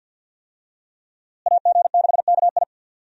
Morse code, heard as a keyed tone, about 48 wpm, 700 Hz. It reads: AG6KI
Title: USA Call Signs - 5 Characters - Repeated 3x - 50wpm